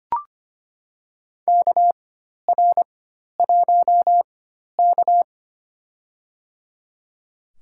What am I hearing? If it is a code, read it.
Morse code, heard as a keyed tone, about 25 wpm, spaced as for 12 wpm, 700 Hz. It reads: KR1K
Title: USA Call Signs - 4 Characters - 4x Character Spacing - 25wpm